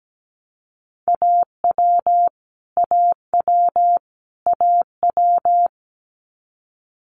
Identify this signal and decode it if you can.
Morse code, heard as a keyed tone, about 17 wpm, 700 Hz. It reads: AW AW AW
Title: Multi-letter Phonemes Repeated 3x 17wpm